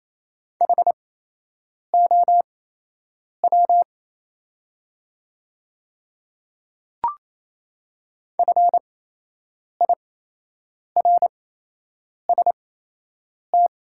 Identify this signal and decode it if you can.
Morse code, heard as a keyed tone, about 27 wpm, spaced as for 8 wpm, 700 Hz. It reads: HOW FIRST
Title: 8x Character Spacing - Top 200 Words - 28wpm